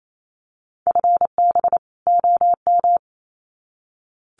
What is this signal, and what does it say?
Morse code, heard as a keyed tone, about 28 wpm, 700 Hz. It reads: FB OM
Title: Common QSO Phrases 28wpm